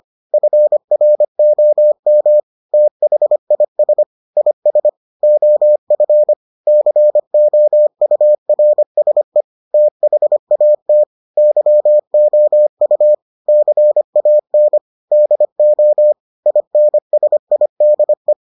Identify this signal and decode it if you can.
Morse code, heard as a keyed tone, about 25 wpm, 600 Hz. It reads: FROM THIS IS OF COURSE THAT YOU CAN DO INSIDE